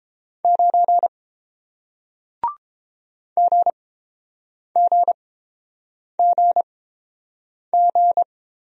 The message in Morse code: 9 GGGG

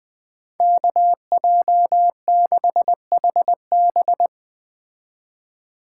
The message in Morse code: KJ6HB

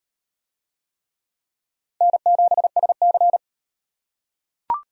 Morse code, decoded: N7SC

